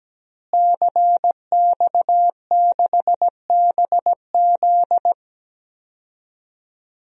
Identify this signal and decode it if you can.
Morse code, heard as a keyed tone, about 17 wpm, 700 Hz. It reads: CX6BZ
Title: DXCC-100 Call Signs - 5 Characters - Repeated 2x - 17wpm